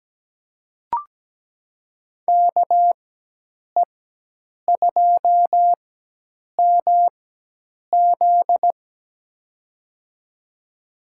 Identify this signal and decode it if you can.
Morse code, heard as a keyed tone, about 17 wpm, spaced as for 8 wpm, 700 Hz. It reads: KE2MZ
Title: USA Call Signs - 5 Characters - 4x Character Spacing - 17wpm